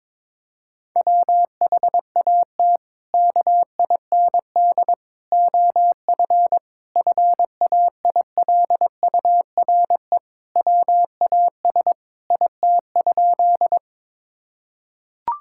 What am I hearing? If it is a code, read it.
Morse code, heard as a keyed tone, about 22 wpm, 700 Hz. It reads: WHAT KIND OF FAILURE WAS IT?
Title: Sentences from Top 1000 Words - Review Part 6 - 22wpm